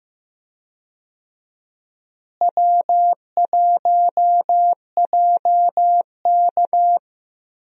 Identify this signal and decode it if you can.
Morse code, heard as a keyed tone, about 15 wpm, 700 Hz. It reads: W1JK
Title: USA Call Signs - 4 Characters - Repeated 3x - 15wpm